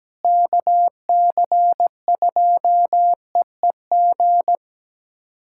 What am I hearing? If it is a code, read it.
Morse code, heard as a keyed tone, about 17 wpm, 700 Hz. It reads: KC2EEG
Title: USA Callsign DE Callsign 17wpm